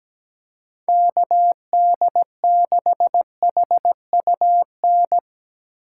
Morse code, heard as a keyed tone, about 17 wpm, 700 Hz. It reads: KD6HUN
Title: USA Call Signs - 6 Characters - 3x Character Spacing - 17wpm